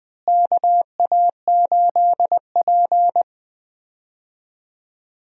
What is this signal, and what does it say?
Morse code, heard as a keyed tone, about 20 wpm, 700 Hz. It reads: KA8P
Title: USA Call Signs - 4 Characters - Repeated 2x - 20wpm